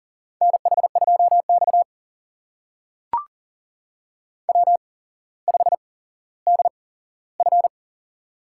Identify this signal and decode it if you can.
Morse code, heard as a keyed tone, about 43 wpm, spaced as for 17 wpm, 700 Hz. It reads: NH2X W5DF
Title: USA Call Signs - 4 Characters - 8x Character Spacing - 40wpm